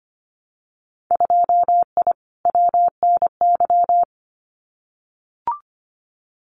Morse code, heard as a keyed tone, about 25 wpm, 700 Hz. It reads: 2I WNY
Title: Winter Field Day Exchanges 25wpm